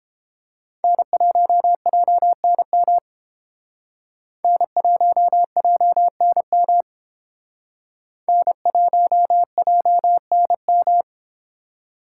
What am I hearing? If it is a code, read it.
Morse code, heard as a keyed tone, about 30 wpm, 700 Hz. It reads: N1JNM N1JNM N1JNM